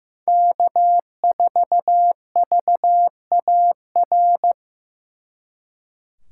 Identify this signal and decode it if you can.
Morse code, heard as a keyed tone, about 18 wpm, 700 Hz. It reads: K4VAR
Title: Speed Racing - USA 1x3 Call Signs 15wpm